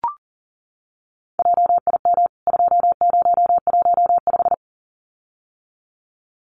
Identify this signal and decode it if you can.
Morse code, heard as a keyed tone, about 40 wpm, 700 Hz. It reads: JIM 2015
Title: CWT Exchange 40wpm